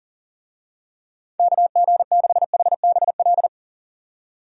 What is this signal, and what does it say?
Morse code, heard as a keyed tone, about 40 wpm, 700 Hz. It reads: KG6HBL
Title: USA Call Signs - 6 Characters - 40wpm